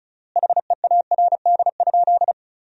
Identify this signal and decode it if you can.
Morse code, heard as a keyed tone, about 36 wpm, 700 Hz. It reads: HEARD?
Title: 3x Word-Spacing - Sentences from Top 400 Words - 35wpm